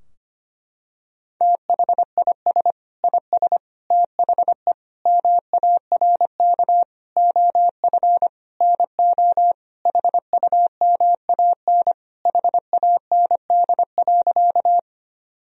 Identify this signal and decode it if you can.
Morse code, heard as a keyed tone, about 25 wpm, 700 Hz. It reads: THIS IS THE MARK OF NO HUMAN HAND.
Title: Sentences from Top 1000 Words - Review Part 3 - 25wpm